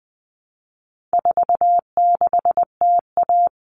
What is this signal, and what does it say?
Morse code, heard as a keyed tone, about 20 wpm, 700 Hz. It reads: 46TA
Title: All Cut Numbers - 4-Digit 20wpm